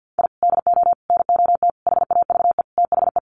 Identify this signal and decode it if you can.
Morse code, heard as a keyed tone, about 50 wpm, 700 Hz. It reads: I DO NOT HAVE THE
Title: Sets of 5 Words 50wpm